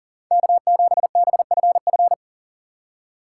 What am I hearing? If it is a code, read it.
Morse code, heard as a keyed tone, about 40 wpm, 700 Hz. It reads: K7BFF
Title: USA Call Signs - 5 Characters - 2x Character Spacing - 40wpm